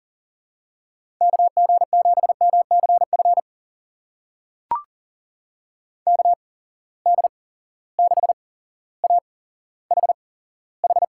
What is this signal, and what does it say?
Morse code, heard as a keyed tone, about 40 wpm, spaced as for 17 wpm, 700 Hz. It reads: KG7MCF KD6AHH
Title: USA Call Signs - 6 Characters - 8x Character Spacing - 40wpm